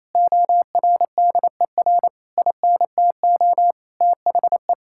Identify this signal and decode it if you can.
Morse code, heard as a keyed tone, about 28 wpm, 700 Hz. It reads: ORDER INTO THE
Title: Sentences from Top 1600 Words 28wpm